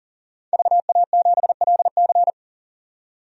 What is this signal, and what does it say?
Morse code, heard as a keyed tone, about 40 wpm, 700 Hz. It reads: VA7LC
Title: DXCC-100 Call Signs - 5 Characters - 40wpm